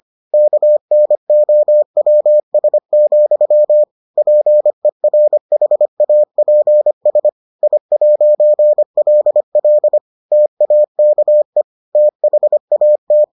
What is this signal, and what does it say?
Morse code, heard as a keyed tone, about 25 wpm, 600 Hz. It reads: KNOWS, PERHAPS I'LL TAKE THAT